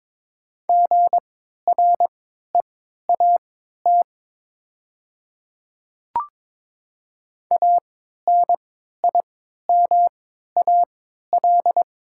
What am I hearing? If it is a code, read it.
Morse code, heard as a keyed tone, about 22 wpm, spaced as for 12 wpm, 700 Hz. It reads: GREAT ANIMAL
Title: 3x Character Spacing - Top 300 Words - 22wpm